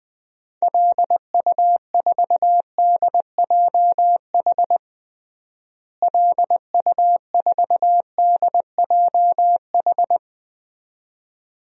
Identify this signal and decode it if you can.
Morse code, heard as a keyed tone, about 20 wpm, 700 Hz. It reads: LU4DJH LU4DJH
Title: DXCC-100 Call Signs - 6 Characters - Repeated 2x - 20wpm